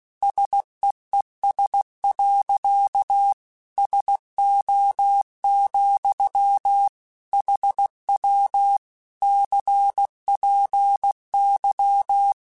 Morse code, heard as a keyed tone, about 16 wpm, 800 Hz. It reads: SEES. SO, HW CPY
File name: QSOs / QSOs-mp3s / WD0FBT 1@15